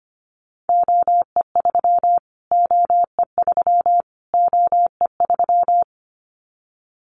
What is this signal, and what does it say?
Morse code, heard as a keyed tone, about 25 wpm, 700 Hz. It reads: OE3 OE3 OE3
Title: Callsign Prefixes for Top 25 DXCC - Repeated 3x - 25wpm